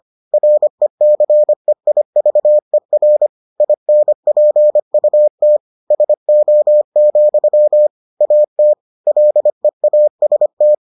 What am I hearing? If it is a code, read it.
Morse code, heard as a keyed tone, about 25 wpm, 600 Hz. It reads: RECEIVER INPUT SO, AT LEAST